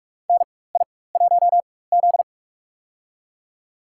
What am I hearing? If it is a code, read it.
Morse code, heard as a keyed tone, about 47 wpm, spaced as for 21 wpm, 700 Hz. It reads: NI1Z
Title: USA Call Signs - 4 Characters - 4x Character Spacing - 45wpm